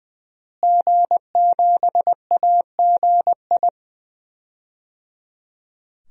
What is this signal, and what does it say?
Morse code, heard as a keyed tone, about 20 wpm, 700 Hz. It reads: G7AGI
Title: DXCC-100 Call Signs - 5 Characters - Repeated 3x - 20wpm